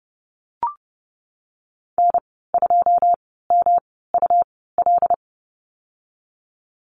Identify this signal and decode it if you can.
Morse code, heard as a keyed tone, about 30 wpm, spaced as for 17 wpm, 700 Hz. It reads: N2MUL
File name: Lesson 19g - Letter w - Call Signs 30wpm